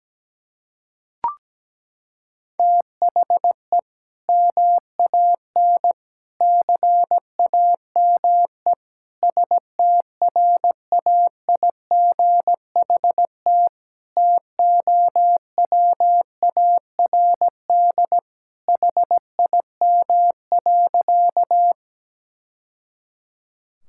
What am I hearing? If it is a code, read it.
Morse code, heard as a keyed tone, about 17 wpm, 700 Hz. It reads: THE MAN CAME STRAIGHT TOWARD HIM.
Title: Sentences from Top 1000 Words - Review Part 6 - 17wpm